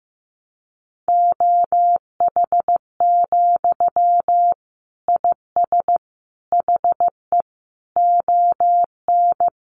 Morse code, heard as a keyed tone, about 15 wpm, 700 Hz. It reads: OH, IS HE ON
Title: Sentences from Top 500 Words 15wpm